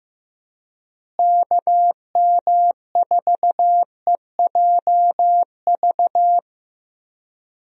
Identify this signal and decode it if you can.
Morse code, heard as a keyed tone, about 15 wpm, 700 Hz. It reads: KM4EJV
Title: USA Call Signs - 6 Characters - Repeated 2x - 15wpm